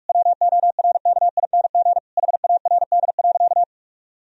Morse code, heard as a keyed tone, about 41 wpm, 700 Hz. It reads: WORKING HARD.
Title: Sentences from Top 1000 Words - Review Part 4 - 45wpm